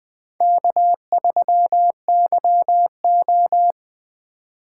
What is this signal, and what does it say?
Morse code, heard as a keyed tone, about 21 wpm, 700 Hz. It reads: K3YO
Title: Speed Racing - USA 1x2 Call Signs 17wpm